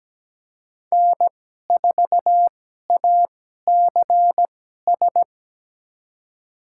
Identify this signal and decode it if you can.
Morse code, heard as a keyed tone, about 17 wpm, spaced as for 12 wpm, 700 Hz. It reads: N4ACS
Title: USA Call Signs - 5 Characters - 2x Character Spacing - 17wpm